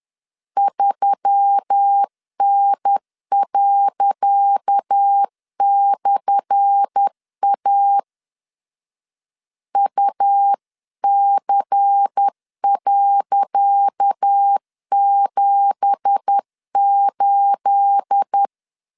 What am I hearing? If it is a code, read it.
Morse code, heard as a keyed tone, about 10 wpm, 800 Hz. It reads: 3N./A UC.78
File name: LEZIONI / 10 SIGNS / 10C DOT 10wpm 2940